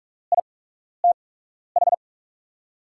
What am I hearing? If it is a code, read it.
Morse code, heard as a keyed tone, about 44 wpm, spaced as for 12 wpm, 700 Hz. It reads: ITH
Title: USA Call Signs - 8x Character Spacing - 45wpm